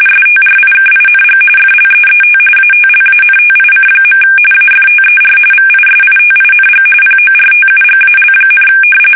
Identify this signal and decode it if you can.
RTTY, about 45.45 baud. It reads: STRIKING FLEET ATLANTIC. AFTER VISITING PORTUGAL,
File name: fsk850 / battleshipnj